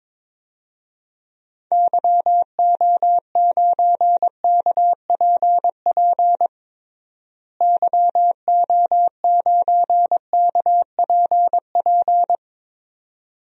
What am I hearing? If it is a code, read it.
Morse code, heard as a keyed tone, about 22 wpm, 700 Hz. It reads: YO9KPP YO9KPP
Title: DXCC-100 Call Signs - 6 Characters - Repeated 2x - 22wpm